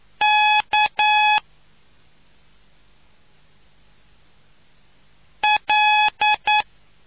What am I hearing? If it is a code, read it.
Morse code, heard as a keyed tone, about 9 wpm, 850 Hz. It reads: K L